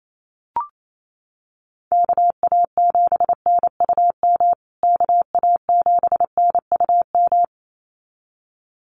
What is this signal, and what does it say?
Morse code, heard as a keyed tone, about 28 wpm, 700 Hz. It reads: KA7NUM KA7NUM
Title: USA Call Signs Repeated 2x 28wpm